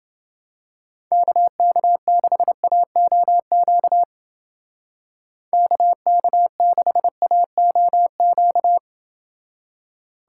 Morse code, heard as a keyed tone, about 28 wpm, 700 Hz. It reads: KK6AOQ KK6AOQ